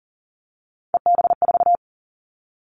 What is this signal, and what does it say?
Morse code, heard as a keyed tone, about 40 wpm, 700 Hz. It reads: EB4
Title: Callsign Prefixes for Top 25 DXCC - 40wpm